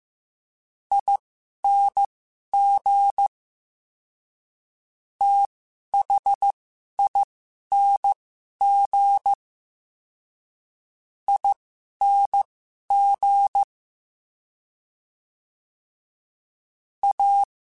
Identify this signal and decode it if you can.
Morse code, heard as a keyed tone, about 15 wpm, spaced as for 11 wpm, 800 Hz. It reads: ING THING ING A
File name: Special INGs-mp3s / Special INGs-mp3s / SlowToFaster ing @10